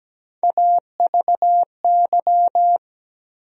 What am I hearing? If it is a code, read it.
Morse code, heard as a keyed tone, about 17 wpm, 700 Hz. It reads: AVY
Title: USA Call Sign Postfixes - 3 Letters 17wpm